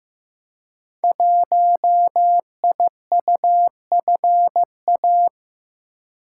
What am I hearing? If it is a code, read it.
Morse code, heard as a keyed tone, about 15 wpm, 700 Hz. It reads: 1IUFA
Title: Code Groups - 5 Letter:Numbers - 15wpm